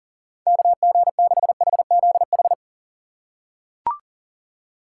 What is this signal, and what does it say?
Morse code, heard as a keyed tone, about 38 wpm, 700 Hz. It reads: KG6HZH
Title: USA Call Signs - 6 Characters - 2x Character Spacing - 40wpm